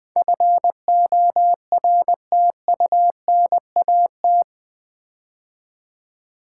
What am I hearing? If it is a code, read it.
Morse code, heard as a keyed tone, about 20 wpm, 700 Hz. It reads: FORTUNAT
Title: Build A Word - 4 Letters At A Time - 11-Letter Words - 20wpm